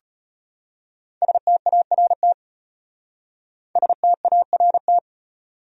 Morse code, heard as a keyed tone, about 32 wpm, 700 Hz. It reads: START START